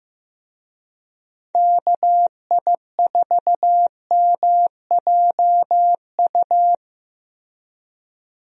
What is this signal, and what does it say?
Morse code, heard as a keyed tone, about 15 wpm, 700 Hz. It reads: KI4MJU
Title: USA Call Signs - 6 Characters - Repeated 3x - 15wpm